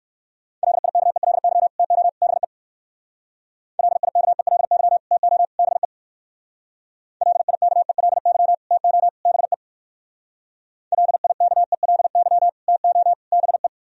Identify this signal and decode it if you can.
Morse code, heard as a keyed tone, about 59 wpm, 700 Hz. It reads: LIKELY TO BE LIKELY TO BE LIKELY TO BE LIKELY TO BE